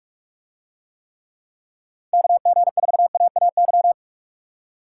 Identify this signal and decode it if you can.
Morse code, heard as a keyed tone, about 49 wpm, 700 Hz. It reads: KG4AAY